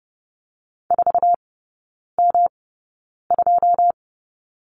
Morse code, heard as a keyed tone, about 30 wpm, spaced as for 9 wpm, 700 Hz. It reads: 4M2
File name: Lesson 18e - Letter m - Three Characters 30wpm